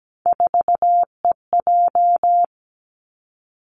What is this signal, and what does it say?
Morse code, heard as a keyed tone, about 17 wpm, 700 Hz. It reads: 4EJ
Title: Three Challenging Letter-Number Combinations 17wpm